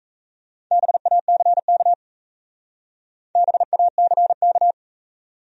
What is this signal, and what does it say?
Morse code, heard as a keyed tone, about 34 wpm, 700 Hz. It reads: BACK BACK
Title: Speed-Racing Top 300 Words 28wpm